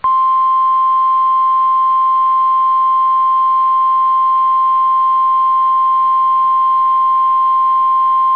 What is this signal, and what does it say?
Emergency Alert System (SAME) tones (the 1050 Hz attention tone). It is heard from 1.1 to 7.3 s.